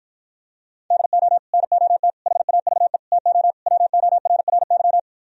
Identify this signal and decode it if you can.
Morse code, heard as a keyed tone, about 53 wpm, 700 Hz. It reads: DO NOT HAVE TO WORRY